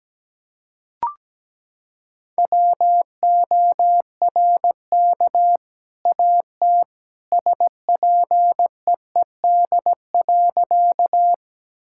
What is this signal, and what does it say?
Morse code, heard as a keyed tone, about 17 wpm, 700 Hz. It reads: WORK AT SPEED.
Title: Sentences from Top 1000 Words - Review Part 5 - 17wpm